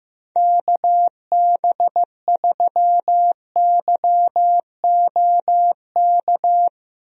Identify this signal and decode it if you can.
Morse code, heard as a keyed tone, about 15 wpm, 700 Hz. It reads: KB3YOK
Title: USA Call Signs - 6 Characters - Repeated 3x - 15wpm